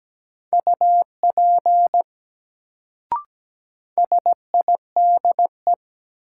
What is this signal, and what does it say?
Morse code, heard as a keyed tone, about 17 wpm, 700 Hz. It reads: UP SIDE
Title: Sets of 2 Words - New Word Drill - 17wpm